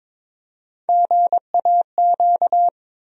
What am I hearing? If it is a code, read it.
Morse code, heard as a keyed tone, about 21 wpm, 700 Hz. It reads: GAQ
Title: USA Call Sign Postfixes - 3 Letters 22wpm